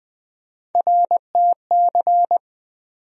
Morse code, heard as a keyed tone, about 20 wpm, 700 Hz. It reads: RTC
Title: USA Call Sign Postfixes - 3 Letters 20wpm